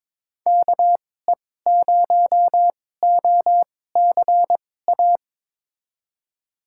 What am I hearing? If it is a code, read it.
Morse code, heard as a keyed tone, about 22 wpm, spaced as for 16 wpm, 700 Hz. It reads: KE0OCA
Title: USA Call Signs - 2x Character Spacing - 22wpm